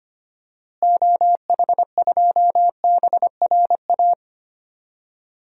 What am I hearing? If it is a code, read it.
Morse code, heard as a keyed tone, about 25 wpm, 700 Hz. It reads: OH2BRA
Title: DXCC-100 Call Signs - 6 Characters - Repeated 2x - 25wpm